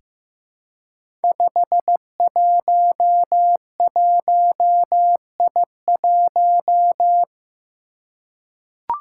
Morse code, heard as a keyed tone, about 15 wpm, 700 Hz. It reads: 511I1
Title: Five Challenging Letter-Number Combinations 15wpm